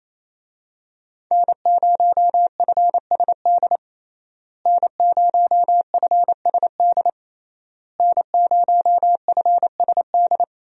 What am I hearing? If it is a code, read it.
Morse code, heard as a keyed tone, about 28 wpm, 700 Hz. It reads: N0FSD N0FSD N0FSD
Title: USA Call Signs - 5 Characters - Repeated 3x - 28wpm